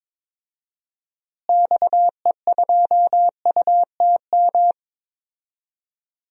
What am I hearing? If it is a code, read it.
Morse code, heard as a keyed tone, about 22 wpm, 700 Hz. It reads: XE2UTM
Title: DXCC-100 Call Signs - 6 Characters - Repeated 2x - 22wpm